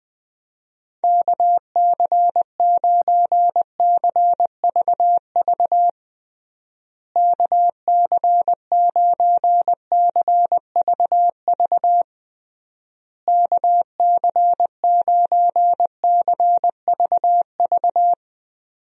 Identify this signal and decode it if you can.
Morse code, heard as a keyed tone, about 19 wpm, 700 Hz. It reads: KC9CVV KC9CVV KC9CVV
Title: USA Call Signs - 6 Characters - Repeated 3x - 20wpm